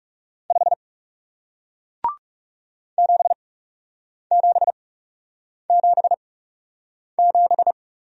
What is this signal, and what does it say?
Morse code, heard as a keyed tone, about 40 wpm, spaced as for 8 wpm, 700 Hz. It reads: 5 7777